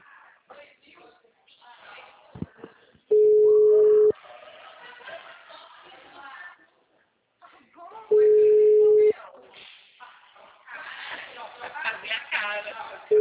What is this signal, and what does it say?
Ringback tone (Europe, 425 Hz), with speech.